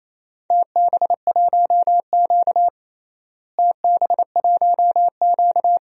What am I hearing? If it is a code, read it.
Morse code, heard as a keyed tone, about 28 wpm, 700 Hz. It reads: TB1Q TB1Q
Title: DXCC-100 Call Signs - 4 Characters - Repeated 2x - 28wpm